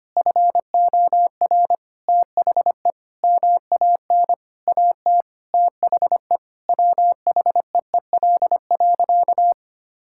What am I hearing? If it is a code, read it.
Morse code, heard as a keyed tone, about 25 wpm, 700 Hz. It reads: FOR THE MAN AT THE WHEEL.
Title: Sentences from Top 1000 Words - Review Part 5 - 25wpm